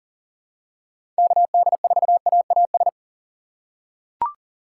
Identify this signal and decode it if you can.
Morse code, heard as a keyed tone, about 40 wpm, 700 Hz. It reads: KD4AAS
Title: USA Call Signs - 4x Character Spacing - 40wpm